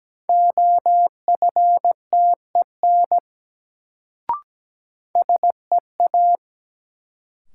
Morse code, heard as a keyed tone, about 17 wpm, 700 Hz. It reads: OFTEN SEA
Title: Sentences from Top 300 Words - New Word Drill - 17wpm